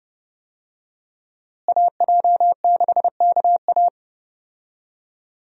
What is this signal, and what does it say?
Morse code, heard as a keyed tone, about 31 wpm, 700 Hz. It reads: AJ6KA
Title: USA Call Signs - 5 Characters - Repeated 2x - 30wpm